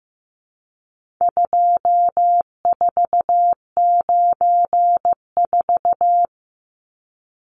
Morse code, heard as a keyed tone, about 15 wpm, 700 Hz. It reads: 2494
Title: Numbers - 4-Digit 15wpm